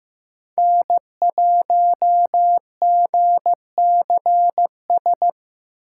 Morse code, heard as a keyed tone, about 15 wpm, 700 Hz. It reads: N1GCS
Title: USA Call Signs - 8x Character Spacing - 15wpm